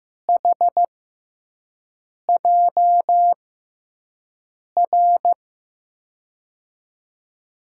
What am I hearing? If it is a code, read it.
Morse code, heard as a keyed tone, about 15 wpm, spaced as for 5 wpm, 700 Hz. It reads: HJR
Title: USA Call Signs - 6 Characters - 6x Character Spacing - 15wpm